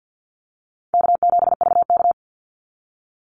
Morse code, heard as a keyed tone, about 50 wpm, 700 Hz. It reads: K7VK
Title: NA SOTA Chaser Callsigns 50wpm